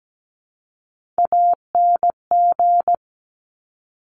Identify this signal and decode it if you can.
Morse code, heard as a keyed tone, about 17 wpm, 700 Hz. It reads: ANG
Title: Top Three Letter Combinations in Words 17wpm